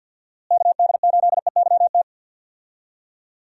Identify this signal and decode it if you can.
Morse code, heard as a keyed tone, about 50 wpm, 700 Hz. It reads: KD8EYT
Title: USA Call Signs - 6 Characters - Repeated 3x - 50wpm